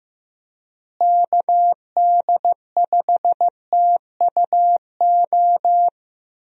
Morse code, heard as a keyed tone, about 16 wpm, 700 Hz. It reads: KD5TUO
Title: USA Call Signs - 6 Characters - Repeated 2x - 15wpm